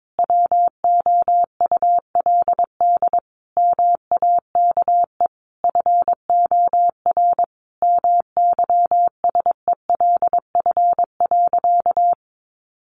Morse code, heard as a keyed tone, about 22 wpm, 700 Hz. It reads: WOULD MAKE FOR MYSELF.
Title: Sentences from Top 1300 Words 22wpm